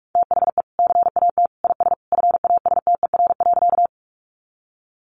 Morse code, heard as a keyed tone, about 44 wpm, 700 Hz. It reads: THE CAT IS FASTER.
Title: Additional Easy Phrases 45wpm